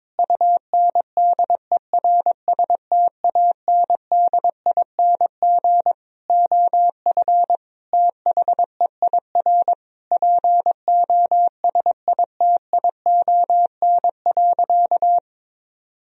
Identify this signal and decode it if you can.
Morse code, heard as a keyed tone, about 22 wpm, 700 Hz. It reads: UNDERSTANDING OF THEIR POSITION.
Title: Sentences from Top 1000 Words - Review Part 3 - 22wpm